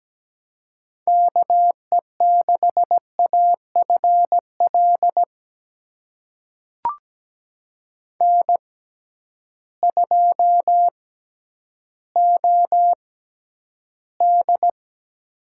Morse code, heard as a keyed tone, about 17 wpm, spaced as for 10 wpm, 700 Hz. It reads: KE6AFL N2OD
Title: USA Call Signs - 6x Character Spacing - 17wpm